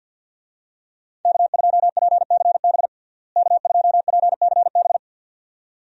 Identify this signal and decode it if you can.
Morse code, heard as a keyed tone, about 51 wpm, 700 Hz. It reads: K2PCB K2PCB
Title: USA Call Signs - 5 Characters - Repeated 2x - 50wpm